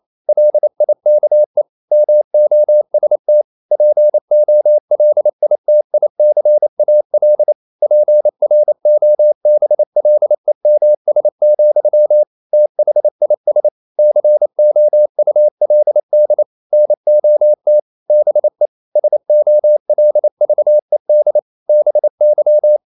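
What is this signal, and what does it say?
Morse code, heard as a keyed tone, about 28 wpm, 600 Hz. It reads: LIKE MOST POLITICAL PROBLEMS, THIS COULD NOT BE SOLVED BY
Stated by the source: Morse code practice at 28 wpm, text of Ken Follett's novel Never in English, from CW med Gnister.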